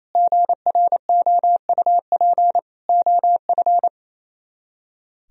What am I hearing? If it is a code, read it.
Morse code, heard as a keyed tone, about 34 wpm, 700 Hz. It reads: GROUP OF
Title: Speed-Racing - Sets of 3 Words 28wpm